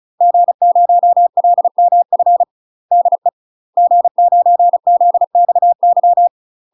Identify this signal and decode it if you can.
Morse code, heard as a keyed tone, about 34 wpm, 700 Hz. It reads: G0LMF DE G9ZXY
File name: UK MASTER QSOs TRAINING / 35WPM / UK MASTER QSOs 0035